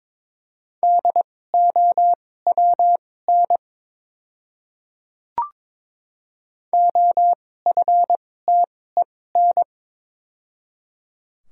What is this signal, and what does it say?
Morse code, heard as a keyed tone, about 22 wpm, spaced as for 16 wpm, 700 Hz. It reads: DOWN OFTEN
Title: 2x Character Spacing - Top 300 Words - 22wpm